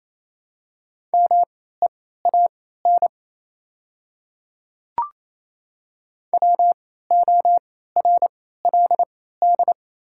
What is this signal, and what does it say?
Morse code, heard as a keyed tone, about 28 wpm, spaced as for 16 wpm, 700 Hz. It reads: MEAN WORLD
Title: 3x Character Spacing - Top 200 Words - 28wpm